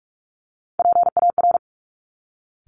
Morse code, heard as a keyed tone, about 46 wpm, 700 Hz. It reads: PAR
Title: Top Three Letter Combinations in Words 45wpm